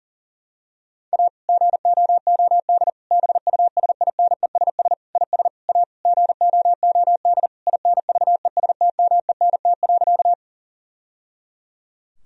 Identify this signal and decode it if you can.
Morse code, heard as a keyed tone, about 40 wpm, 700 Hz. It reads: A GOOD BUSINESS IS A GOOD INVESTMENT.
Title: Sentences from Top 1000 Words - Review Part 4 - 40wpm